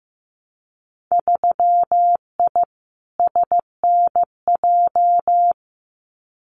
Morse code, heard as a keyed tone, about 15 wpm, 700 Hz. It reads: 3I SNJ
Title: Winter Field Day Exchanges Encore 15wpm